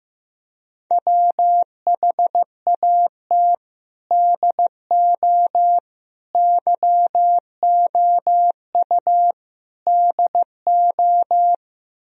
Morse code, heard as a keyed tone, about 15 wpm, 700 Hz. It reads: WHAT DO YOU DO
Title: Speed-Racing - Sets of 4 Words 15wpm